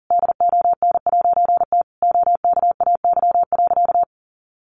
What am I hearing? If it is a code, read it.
Morse code, heard as a keyed tone, about 40 wpm, 700 Hz. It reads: DON'T OKAY.
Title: A Sound Of Thunder 40wpm